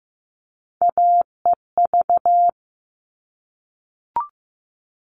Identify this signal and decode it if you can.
Morse code, heard as a keyed tone, about 15 wpm, 700 Hz. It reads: AEV
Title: All Cut Numbers - 3-Digit 15wpm